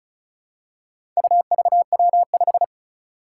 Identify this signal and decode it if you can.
Morse code, heard as a keyed tone, about 36 wpm, 700 Hz. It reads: UVW5
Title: Four Challenging Letter-Number Combinations 35wpm